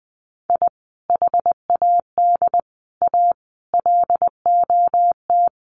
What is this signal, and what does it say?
Morse code, heard as a keyed tone, about 20 wpm, 700 Hz. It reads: I HAD A LOT
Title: Sets of 5 Words Encore 20wpm